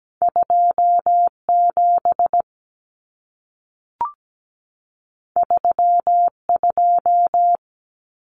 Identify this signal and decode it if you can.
Morse code, heard as a keyed tone, about 17 wpm, 700 Hz. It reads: 27 32
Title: Numbers - 2-Digit 17wpm